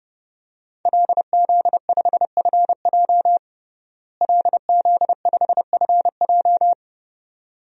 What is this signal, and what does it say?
Morse code, heard as a keyed tone, about 32 wpm, 700 Hz. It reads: LZ5FJ LZ5FJ
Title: DXCC-100 Call Signs - 5 Characters - Repeated 2x - 30wpm